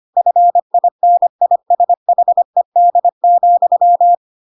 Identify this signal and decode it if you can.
Morse code, heard as a keyed tone, about 25 wpm, 700 Hz. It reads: FINISHED,
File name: Pinocchio / Pinocchio ch8 25 25 ik1hge